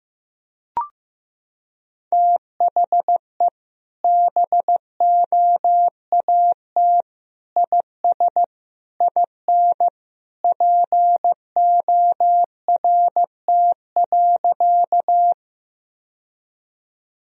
Morse code, heard as a keyed tone, about 15 wpm, 700 Hz. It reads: THE BOAT IS IN PORT.
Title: Sentences from Top 1000 Words - Review Part 1 - 15wpm